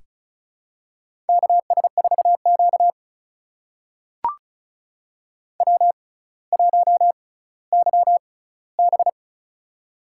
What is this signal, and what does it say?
Morse code, heard as a keyed tone, about 36 wpm, spaced as for 18 wpm, 700 Hz. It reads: KS4Q W1YB
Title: USA Call Signs - 4 Characters - 6x Character Spacing - 35wpm